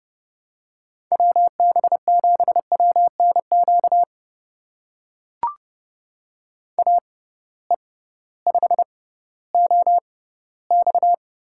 Morse code, heard as a keyed tone, about 31 wpm, spaced as for 16 wpm, 700 Hz. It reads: WB7WNQ AE5OX
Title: USA Call Signs - 6x Character Spacing - 30wpm